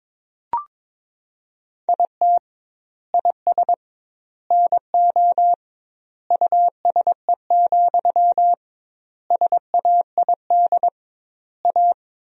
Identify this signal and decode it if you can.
Morse code, heard as a keyed tone, about 22 wpm, 700 Hz. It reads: IT IS NO USE, SAID A
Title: 2x Word-Spacing - Sentences from Top 500 Words - 22wpm